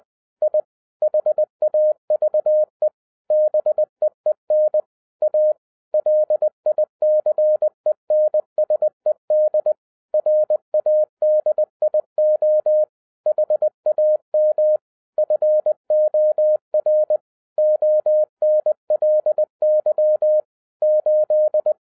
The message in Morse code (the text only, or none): I HAVE BEEN A LICENSED RADIO HAM FOR ONLY 8